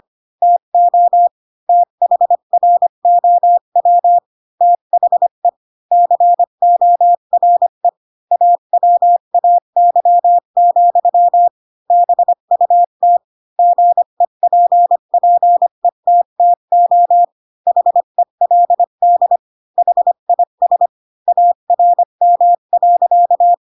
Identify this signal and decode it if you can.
Morse code, heard as a keyed tone, about 25 wpm, 700 Hz. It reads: TO THROW THE CORE AWAY, BUT GEPPETTO HELD HIS ARM.
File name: Pinocchio / Pinocchio ch7 20 25 ik1hge